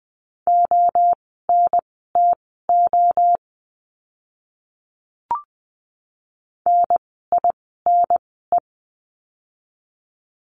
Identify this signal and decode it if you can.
Morse code, heard as a keyed tone, about 20 wpm, spaced as for 14 wpm, 700 Hz. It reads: ONTO NINE
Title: 030 Lesson 6f - Letter i - Words 20wpm